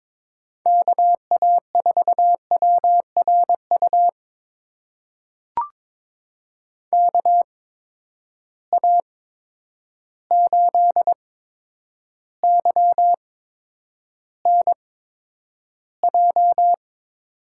Morse code, heard as a keyed tone, about 22 wpm, spaced as for 9 wpm, 700 Hz. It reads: KA4WRU KA8YNJ
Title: USA Call Signs - 6 Characters - 8x Character Spacing - 22wpm